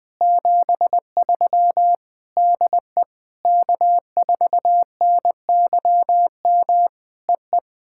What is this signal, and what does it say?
Morse code, heard as a keyed tone, about 20 wpm, 700 Hz. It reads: 73 DE K4NYM EE
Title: POTA QSO Practice - 20wpm